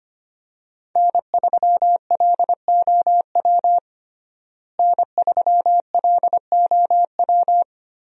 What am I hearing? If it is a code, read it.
Morse code, heard as a keyed tone, about 25 wpm, 700 Hz. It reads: N3LOW N3LOW
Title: USA Call Signs - 5 Characters - Repeated 3x - 25wpm